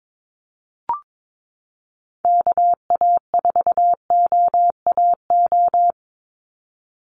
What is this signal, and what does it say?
Morse code, heard as a keyed tone, about 22 wpm, 700 Hz. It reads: KA4OAO
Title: USA Call Signs 22wpm